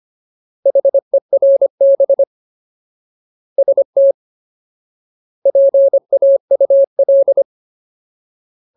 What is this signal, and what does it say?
Morse code, heard as a keyed tone, about 25 wpm, 550 Hz. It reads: HERB ST PAUL
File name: Ultimate Word List / 25WPM Ultimate